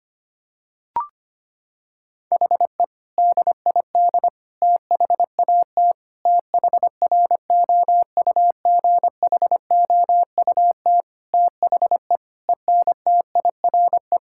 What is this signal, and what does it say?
Morse code, heard as a keyed tone, about 25 wpm, 700 Hz. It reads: HE DID THAT THROUGHOUT THE ENTIRE
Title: Sentences from Top 1000 Words - Review Part 5 - 25wpm